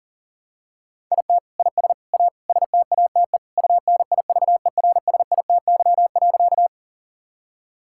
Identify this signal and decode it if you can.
Morse code, heard as a keyed tone, about 40 wpm, 700 Hz. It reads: IT IS A STATE UNIVERSITY.
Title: Sentences from Top 1000 Words - Review Part 5 - 40wpm